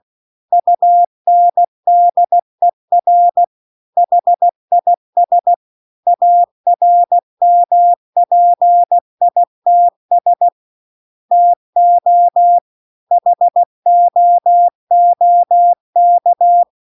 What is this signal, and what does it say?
Morse code, heard as a keyed tone, about 16 wpm, 700 Hz. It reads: UNDER HIS ARMPITS TO HOOK
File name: CW-The-Big-Twoheaded-River-2-16WPM